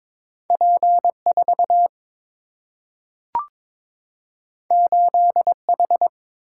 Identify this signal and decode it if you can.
Morse code, heard as a keyed tone, about 21 wpm, 700 Hz. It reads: P4 8H
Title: Single-Letter Single-Number 22wpm